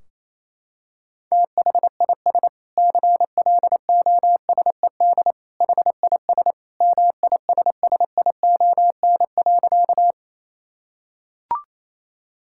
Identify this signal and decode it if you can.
Morse code, heard as a keyed tone, about 28 wpm, 700 Hz. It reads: THIS CLOSED HIS MISSION.
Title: Sentences from Top 1600 Words 28wpm